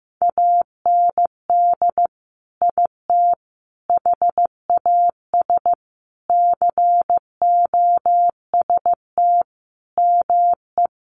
Morse code, heard as a keyed tone, about 15 wpm, 700 Hz. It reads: AND IT HAS COST ME
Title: Sentences from Top 1300 Words 15wpm